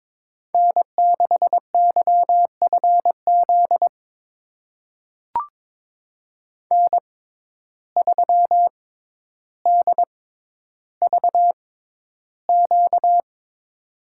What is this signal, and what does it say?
Morse code, heard as a keyed tone, about 22 wpm, spaced as for 11 wpm, 700 Hz. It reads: N6YFZ N3DVQ
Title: USA Call Signs - 5 Characters - 6x Character Spacing - 22wpm